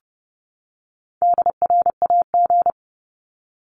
Morse code, heard as a keyed tone, about 30 wpm, 700 Hz. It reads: DRAG
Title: Top 1251-1500 Words 30wpm